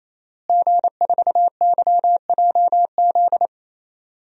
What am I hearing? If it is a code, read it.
Morse code, heard as a keyed tone, about 28 wpm, 700 Hz. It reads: G4YJZ
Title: DXCC-100 Call Signs - 5 Characters - Repeated 3x - 28wpm